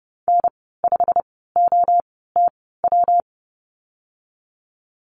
Morse code, heard as a keyed tone, about 30 wpm, spaced as for 17 wpm, 700 Hz. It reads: N5OTW
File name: Lesson 19g - Letter w - Call Signs 30wpm